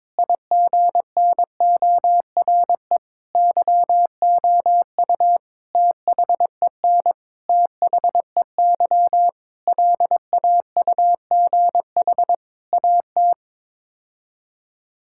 Morse code, heard as a keyed tone, about 22 wpm, 700 Hz. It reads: IGNORE YOU THEN THEY LAUGH AT
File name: practice / 22wpm / Sets of 8 Words - 22wpm